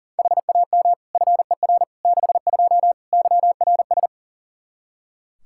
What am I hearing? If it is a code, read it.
Morse code, heard as a keyed tone, about 40 wpm, 700 Hz. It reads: HAM FER 62 YRS
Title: K1USN SST QSO Practice 40wpm